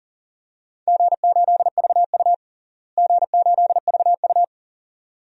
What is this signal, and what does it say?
Morse code, heard as a keyed tone, about 40 wpm, 700 Hz. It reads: G8VU G8VU
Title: DXCC-100 Call Signs - 4 Characters - Repeated 2x - 40wpm